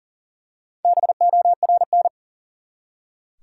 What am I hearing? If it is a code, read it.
Morse code, heard as a keyed tone, about 40 wpm, 700 Hz. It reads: BORN
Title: Four Letter Words - 40wpm